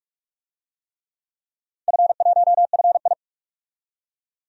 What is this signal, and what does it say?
Morse code, heard as a keyed tone, about 41 wpm, 700 Hz. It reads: F1FI
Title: DXCC-100 Call Signs - 4 Characters - 45wpm